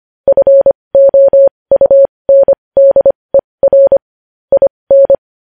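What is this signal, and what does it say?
Morse code, heard as a keyed tone, about 25 wpm, spaced as for 20 wpm, 550 Hz. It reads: FOUNDER IN
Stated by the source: Morse code news headlines sent at 20 words per minute.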